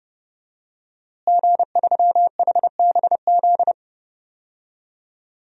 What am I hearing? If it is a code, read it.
Morse code, heard as a keyed tone, about 30 wpm, 700 Hz. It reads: G3HBZ
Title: DXCC-100 Call Signs - 5 Characters - Repeated 2x - 30wpm